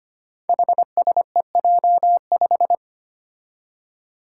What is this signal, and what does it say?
Morse code, heard as a keyed tone, about 25 wpm, 700 Hz. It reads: HSEJ5
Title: Five Challenging Letter-Number Combinations 25wpm